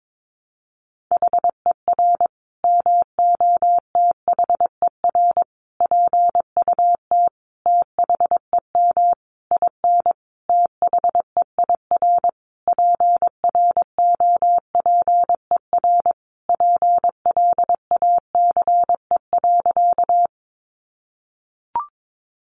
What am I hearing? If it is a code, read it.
Morse code, heard as a keyed tone, about 22 wpm, 700 Hz. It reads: HER MOTHER PUT THEM IN THEIR PROPER PLACE.
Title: Sentences from Top 800 Words 22wpm